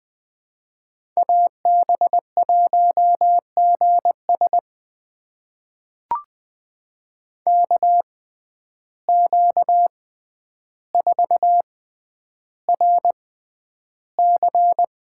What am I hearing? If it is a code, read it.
Morse code, heard as a keyed tone, about 20 wpm, spaced as for 10 wpm, 700 Hz. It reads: AB1GS KQ4RC
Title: USA Call Signs - 5 Characters - 6x Character Spacing - 20wpm